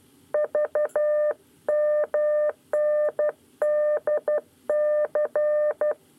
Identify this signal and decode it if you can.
Morse code, heard as a keyed tone, about 12 wpm, 600 Hz. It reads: VMNDC
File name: 9 lezione cw 2